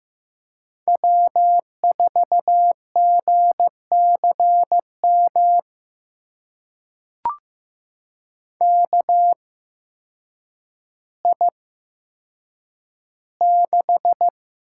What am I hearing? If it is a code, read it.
Morse code, heard as a keyed tone, about 15 wpm, spaced as for 8 wpm, 700 Hz. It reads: W4GCM KI6
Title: USA Call Signs - 8x Character Spacing - 15wpm